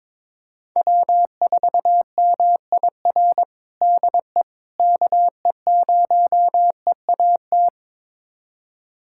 Morse code, heard as a keyed tone, about 22 wpm, 700 Hz. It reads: W4MIR DE KE0EAT
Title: USA Callsign DE Callsign 22wpm